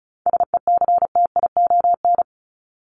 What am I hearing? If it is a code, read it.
Morse code, heard as a keyed tone, about 34 wpm, 700 Hz. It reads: SECTION
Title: Top 501-750 Words 35wpm